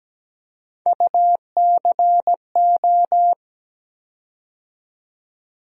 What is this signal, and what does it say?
Morse code, heard as a keyed tone, about 17 wpm, 700 Hz. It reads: UCO
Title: USA Call Sign Postfixes - 3 Letters 17wpm